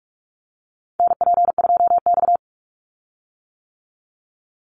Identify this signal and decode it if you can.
Morse code, heard as a keyed tone, about 44 wpm, 700 Hz. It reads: NP2X
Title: Call Signs - 45wpm